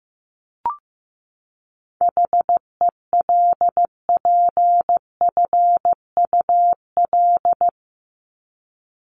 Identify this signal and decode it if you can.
Morse code, heard as a keyed tone, about 15 wpm, 700 Hz. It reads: HELPFUL
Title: Top 751-1000 Words 15wpm